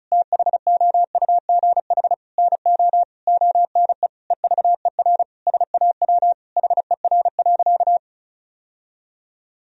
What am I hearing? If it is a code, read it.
Morse code, heard as a keyed tone, about 36 wpm, 700 Hz. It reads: THOUGH NO ONE EVER SAW HER.
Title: Sentences from Top 1000 Words - Review Part 2 - 35wpm